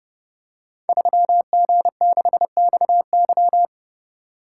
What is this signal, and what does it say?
Morse code, heard as a keyed tone, about 30 wpm, 700 Hz. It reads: 3G6XY
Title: Code Groups - 5 Letter:Numbers - 30wpm